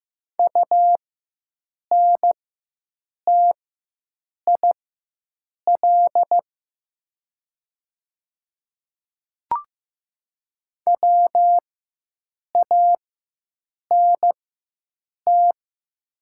Morse code, heard as a keyed tone, about 15 wpm, spaced as for 7 wpm, 700 Hz. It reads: UNTIL WANT
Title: 4x Character Spacing - Top 300 Words - 15wpm